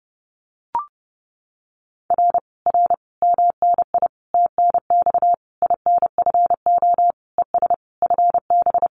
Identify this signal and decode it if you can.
Morse code, heard as a keyed tone, about 30 wpm, 700 Hz. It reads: R R MNI TNX INFO ES FB
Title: Common QSO Phrases Encore 30wpm